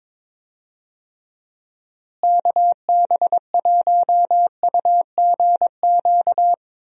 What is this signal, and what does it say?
Morse code, heard as a keyed tone, about 21 wpm, 700 Hz. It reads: KB1UGQ
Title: USA Call Signs - 6 Characters - Repeated 3x - 22wpm